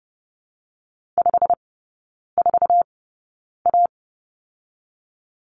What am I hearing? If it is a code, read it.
Morse code, heard as a keyed tone, about 30 wpm, 700 Hz. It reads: 5 4 A
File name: Lesson 19e - Letter w - Three Characters 30wpm